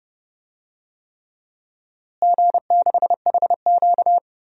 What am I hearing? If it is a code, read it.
Morse code, heard as a keyed tone, about 30 wpm, 700 Hz. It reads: G6HQ
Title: DXCC-100 Call Signs - 4 Characters - Repeated 3x - 30wpm